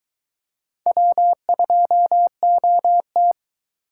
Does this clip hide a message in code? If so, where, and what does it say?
Yes: Morse code from 0.9 to 3.3 s, heard as a keyed tone, about 22 wpm, 700 Hz. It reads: W2OT